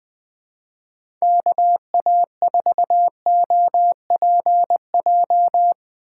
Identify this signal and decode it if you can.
Morse code, heard as a keyed tone, about 20 wpm, 700 Hz. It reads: KA4OPJ
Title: USA Call Signs - 3x Character Spacing - 20wpm